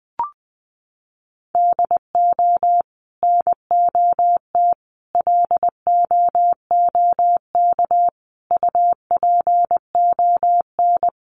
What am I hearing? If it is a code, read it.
Morse code, heard as a keyed tone, about 20 wpm, 700 Hz. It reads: DO NOT LOOK UPON
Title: Sentences from Top 1100 Words 20wpm